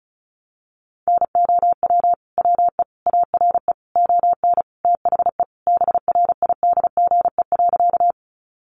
Morse code, heard as a keyed tone, about 34 wpm, 700 Hz. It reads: NOW WE ARE ON THE BRIDGE.
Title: Sentences from Top 1200 Words 35wpm